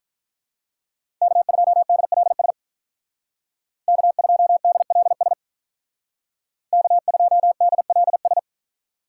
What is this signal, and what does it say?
Morse code, heard as a keyed tone, about 44 wpm, 700 Hz. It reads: K2DLS K2DLS K2DLS
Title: Speed Racing - USA 1x3 Call Signs 35wpm